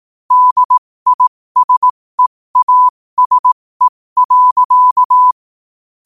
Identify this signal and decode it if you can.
Morse code, heard as a keyed tone, about 18 wpm, spaced as for 15 wpm, 1000 Hz. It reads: DISEASE.
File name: 15wpm